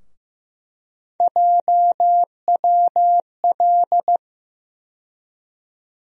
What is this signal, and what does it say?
Morse code, heard as a keyed tone, about 15 wpm, 700 Hz. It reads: JWL
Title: USA Call Sign Postfixes - 3 Letters 15wpm